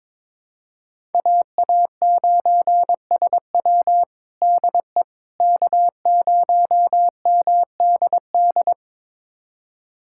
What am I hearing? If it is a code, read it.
Morse code, heard as a keyed tone, about 22 wpm, 700 Hz. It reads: AA9SW DE K0MDD
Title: USA Callsign DE Callsign 22wpm